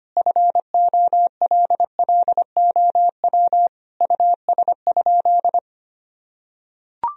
Morse code, heard as a keyed tone, about 25 wpm, 700 Hz. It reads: FOLLOW US?
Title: Sentences from Top 1000 Words - Review Part 1 - 25wpm